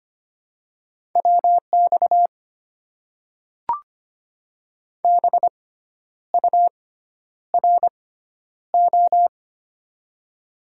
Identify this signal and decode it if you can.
Morse code, heard as a keyed tone, about 25 wpm, spaced as for 10 wpm, 700 Hz. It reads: WX BURO
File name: Top 90 QSO Elements - 6x Character Spacing - 25wpm